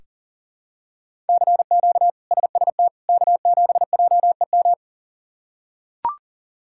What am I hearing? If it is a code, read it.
Morse code, heard as a keyed tone, about 40 wpm, 700 Hz. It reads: CQ SST K7JEM
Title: K1USN SST QSO Practice - 40wpm